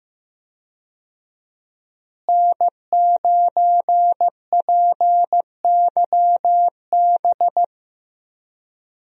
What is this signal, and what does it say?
Morse code, heard as a keyed tone, about 15 wpm, 700 Hz. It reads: N9PYB
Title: USA Call Signs - 5 Characters - Repeated 3x - 15wpm